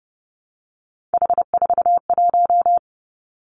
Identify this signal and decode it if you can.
Morse code, heard as a keyed tone, about 30 wpm, 700 Hz. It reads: H41
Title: Three Challenging Letter-Number Combinations 30wpm